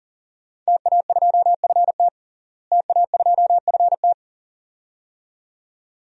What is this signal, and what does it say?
Morse code, heard as a keyed tone, about 37 wpm, 700 Hz. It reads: TA2FT TA2FT
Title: DXCC-100 Call Signs - 5 Characters - Repeated 2x - 40wpm